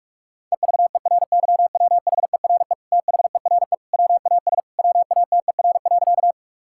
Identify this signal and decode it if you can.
Morse code, heard as a keyed tone, about 46 wpm, 700 Hz. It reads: EVERYWHERE THERE WAS WATER.
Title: Sentences from Top 1600 Words 45wpm